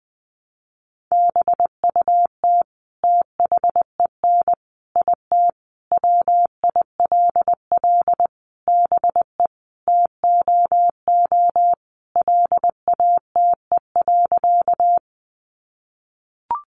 Morse code, heard as a keyed tone, about 20 wpm, 700 Hz. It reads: BUT THEN IT WILL BE TOO LATE.
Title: Sentences from Top 300 Words 20wpm